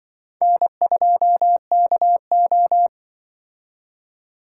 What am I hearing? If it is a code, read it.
Morse code, heard as a keyed tone, about 23 wpm, 700 Hz. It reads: N2KO
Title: Speed Racing - USA 1x2 Call Signs 20wpm